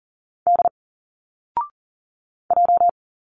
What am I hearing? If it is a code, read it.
Morse code, heard as a keyed tone, about 39 wpm, 700 Hz. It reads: D J
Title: Single Letters 40wpm